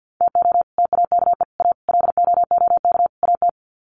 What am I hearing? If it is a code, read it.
Morse code, heard as a keyed tone, about 50 wpm, 700 Hz. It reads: TO TAKE A LOOK AT
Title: Sets of 5 Words 50wpm